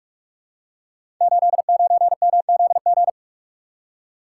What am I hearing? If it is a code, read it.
Morse code, heard as a keyed tone, about 44 wpm, 700 Hz. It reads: 89MZG
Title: Code Groups - 5 Letter:Numbers - 45wpm